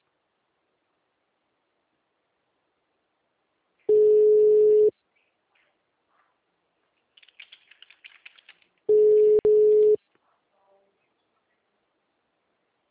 Ringback tone (Europe, 425 Hz).
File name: out- -2026- - - .25061